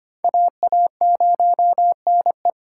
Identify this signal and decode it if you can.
Morse code, heard as a keyed tone, about 33 wpm, 700 Hz. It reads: AA0NE